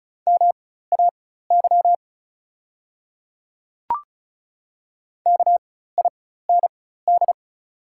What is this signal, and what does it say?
Morse code, heard as a keyed tone, about 36 wpm, spaced as for 16 wpm, 700 Hz. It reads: MAY KIND
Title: 4x Character Spacing - Top 200 Words - 35wpm